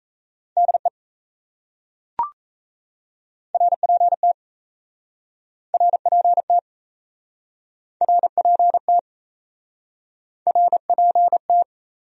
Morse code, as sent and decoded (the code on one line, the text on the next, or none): -.. / .-. .--. - / .-. .--. - / .-. .--. - / .-. .--. -
DE RPT RPT RPT RPT